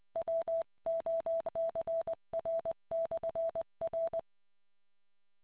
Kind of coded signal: Morse code